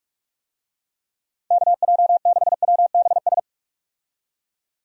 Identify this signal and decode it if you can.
Morse code, heard as a keyed tone, about 44 wpm, 700 Hz. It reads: KJ6WBS
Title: USA Call Signs - 6 Characters - 45wpm